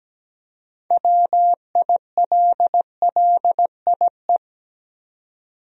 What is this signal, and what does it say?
Morse code, heard as a keyed tone, about 17 wpm, 700 Hz. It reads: WILLIE
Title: USA Common Names - 17wpm